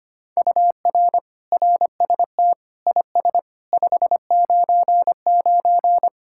Morse code, heard as a keyed tone, about 33 wpm, 700 Hz. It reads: UR RST IS 599